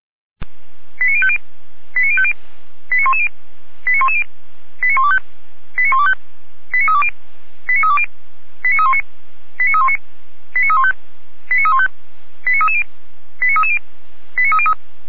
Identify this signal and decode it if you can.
Five-tone selective call (ZVEI), heard as quick runs of five tones, tones of about 70 ms. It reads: 89050 89050 89110 89110 89125 89125 89320 89320 89319 89319 89315 89315 89330 89330 89303